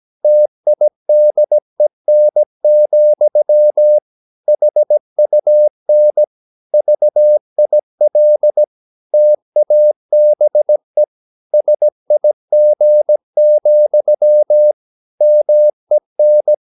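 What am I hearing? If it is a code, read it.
Morse code, heard as a keyed tone, about 17 wpm, 600 Hz. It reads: TIDEN, HUN VIL TABE SIG, MEN
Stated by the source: Slow (QRS) Morse practice at 17 wpm, from CW med Gnister.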